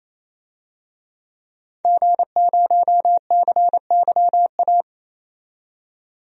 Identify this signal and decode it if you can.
Morse code, heard as a keyed tone, about 28 wpm, 700 Hz. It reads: G0CYA
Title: DXCC-100 Call Signs - 5 Characters - 28wpm